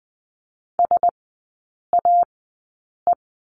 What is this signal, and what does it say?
Morse code, heard as a keyed tone, about 20 wpm, spaced as for 8 wpm, 700 Hz. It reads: SAE
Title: 121 Lesson 20e - Number 3 - Three Characters 20wpm